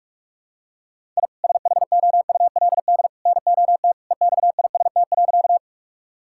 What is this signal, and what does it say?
Morse code, heard as a keyed tone, about 41 wpm, 700 Hz. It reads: I SHOULD NOT EXIST.
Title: Sentences from Top 1000 Words - Review Part 5 - 45wpm